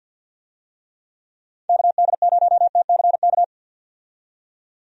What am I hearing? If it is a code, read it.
Morse code, heard as a keyed tone, about 46 wpm, 700 Hz. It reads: KD0TCK
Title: USA Call Signs - 6 Characters - 50wpm